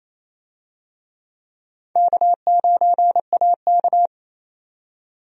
Morse code, heard as a keyed tone, about 28 wpm, 700 Hz. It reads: K9AK
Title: USA Call Signs - 4 Characters - 28wpm